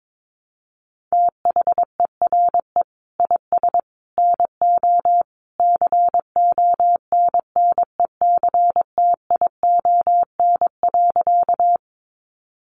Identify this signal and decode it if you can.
Morse code, heard as a keyed tone, about 22 wpm, 700 Hz. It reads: THERE IS NO CONNECTION.
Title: Sentences from Top 1300 Words 22wpm